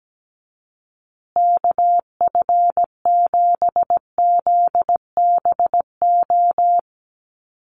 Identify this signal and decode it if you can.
Morse code, heard as a keyed tone, about 17 wpm, 700 Hz. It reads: KF7ZBO
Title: USA Call Signs Repeated 3x 17wpm